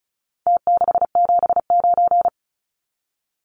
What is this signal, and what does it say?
Morse code, heard as a keyed tone, about 34 wpm, 700 Hz. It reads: T679
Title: Numbers - Contest Style - 4-Digit 35wpm